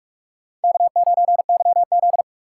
Morse code, heard as a keyed tone, about 46 wpm, 700 Hz. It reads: K9YZ
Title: USA Call Signs - 3x Character Spacing - 45wpm